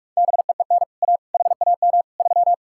Morse code, heard as a keyed tone, about 59 wpm, 700 Hz. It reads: BEEN A HAM 3